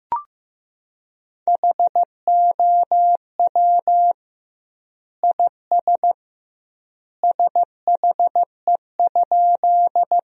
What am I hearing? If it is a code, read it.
Morse code, heard as a keyed tone, about 15 wpm, 700 Hz. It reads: HOW IS SHE?
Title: 2x Word-Spacing - Sentences from Top 100 Words - 15wpm